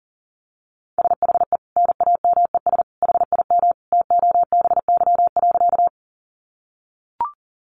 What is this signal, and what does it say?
Morse code, heard as a keyed tone, about 40 wpm, 700 Hz. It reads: SHE NAMES HIM TOBY.
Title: Easy Phrases 40wpm